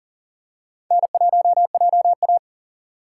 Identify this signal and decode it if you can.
Morse code, heard as a keyed tone, about 40 wpm, 700 Hz. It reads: N1JA
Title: USA Call Signs - 4 Characters - 3x Character Spacing - 40wpm